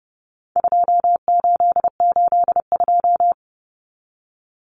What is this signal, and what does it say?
Morse code, heard as a keyed tone, about 30 wpm, 700 Hz. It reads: 2882
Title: Common Cut Numbers - 4-Digit 30wpm